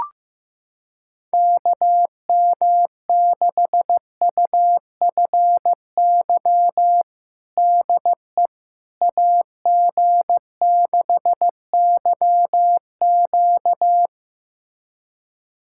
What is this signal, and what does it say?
Morse code, heard as a keyed tone, about 15 wpm, 700 Hz. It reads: KM6UFY DE AG6YQ
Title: USA Callsign DE Callsign 15wpm